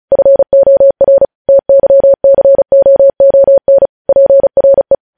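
Morse code, heard as a keyed tone, about 36 wpm, 550 Hz. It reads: FOR TYCOON PRE
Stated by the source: Morse code practice sending news headlines at 35 wpm.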